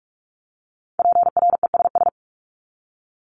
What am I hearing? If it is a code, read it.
Morse code, heard as a keyed tone, about 44 wpm, 700 Hz. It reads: PRESS
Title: Top 500 Words 45wpm